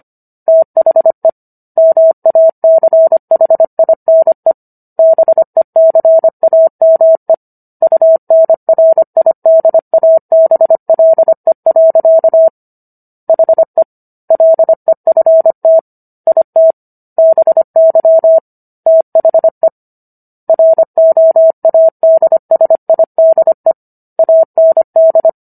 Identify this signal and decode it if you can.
Morse code, heard as a keyed tone, about 25 wpm, 650 Hz. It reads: THE MACHINE BECAME UNRIDABLE. HE LEFT IT BY THE ROADSIDE AND